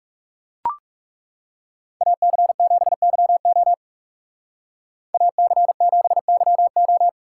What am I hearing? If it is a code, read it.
Morse code, heard as a keyed tone, about 35 wpm, 700 Hz. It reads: AC7YO AC7YO
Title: Speed Racing - USA 2x2 Call Signs 30wpm